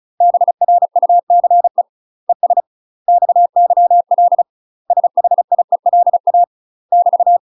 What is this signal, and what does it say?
Morse code, heard as a keyed tone, about 34 wpm, 700 Hz. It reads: BRUCE ES XYL SHIELA <BT>
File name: UK MASTER QSOs TRAINING / 35WPM / UK MASTER QSOs 0001